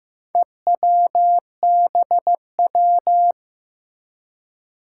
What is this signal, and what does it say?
Morse code, heard as a keyed tone, about 15 wpm, 700 Hz. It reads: EWBW
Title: Four Challenging Letter-Number Combinations 15wpm